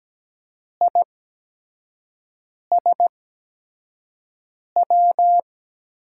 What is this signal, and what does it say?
Morse code, heard as a keyed tone, about 17 wpm, spaced as for 5 wpm, 700 Hz. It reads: ISW
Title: USA Call Signs - 6 Characters - 8x Character Spacing - 17wpm